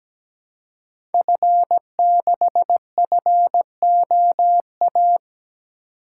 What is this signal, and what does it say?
Morse code, heard as a keyed tone, about 17 wpm, 700 Hz. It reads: F6FOA
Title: Code Groups - 5 Letter:Numbers - 17wpm